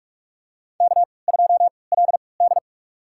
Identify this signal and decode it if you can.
Morse code, heard as a keyed tone, about 48 wpm, spaced as for 26 wpm, 700 Hz. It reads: K2LD
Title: USA Call Signs - 4 Characters - 3x Character Spacing - 45wpm